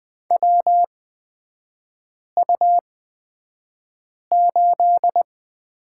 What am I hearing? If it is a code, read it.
Morse code, heard as a keyed tone, about 20 wpm, spaced as for 5 wpm, 700 Hz. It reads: WU8
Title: 243 Lesson 39e - Letter z - Three Characters 20wpm